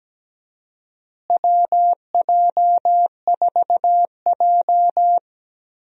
Morse code, heard as a keyed tone, about 17 wpm, 700 Hz. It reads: WJ4J
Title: USA Call Signs - 4 Characters - Repeated 3x - 17wpm